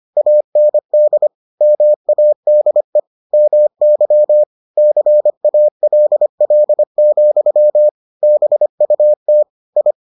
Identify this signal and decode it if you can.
Morse code, heard as a keyed tone, about 25 wpm, 600 Hz. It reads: AND MADE MY CALL, BUT I